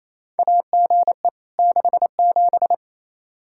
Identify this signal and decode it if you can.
Morse code, heard as a keyed tone, about 27 wpm, 700 Hz. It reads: AGE 67
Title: Practice QSO - Exchange 3 - More Info - 28wpm